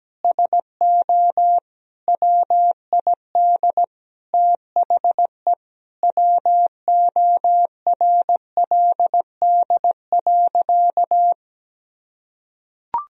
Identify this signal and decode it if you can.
Morse code, heard as a keyed tone, about 17 wpm, 700 Hz. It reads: SO WID THE WORLD.
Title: Sentences from Top 200 Words and Abbreviations 17wpm